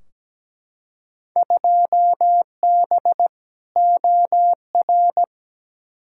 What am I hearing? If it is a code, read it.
Morse code, heard as a keyed tone, about 17 wpm, 700 Hz. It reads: 2B OR
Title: Field Day Exchanges Encore - 17wpm